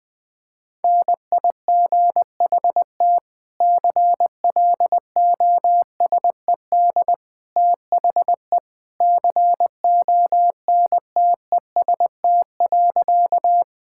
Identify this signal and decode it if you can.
Morse code, heard as a keyed tone, about 20 wpm, 700 Hz. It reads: NIGHT CLOSED THE CONTEST.
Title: Sentences from Top 1600 Words 20wpm